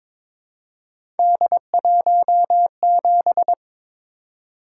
Morse code, heard as a keyed tone, about 22 wpm, 700 Hz. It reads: D17
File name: practice / 22wpm / Three Letter-Number Combinations - 22wpm